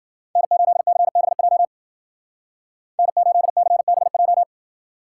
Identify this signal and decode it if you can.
Morse code, heard as a keyed tone, about 51 wpm, 700 Hz. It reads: N8CBJ N8CBJ